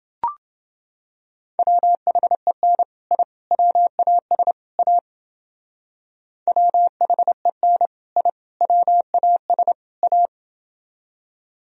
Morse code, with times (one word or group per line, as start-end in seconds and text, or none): 1.6-2.8 WHEN
3.1-3.2 I
3.5-4.5 WAS
4.8-5.0 A
6.5-7.9 WHEN
8.2-8.3 I
8.6-9.7 WAS
10.0-10.3 A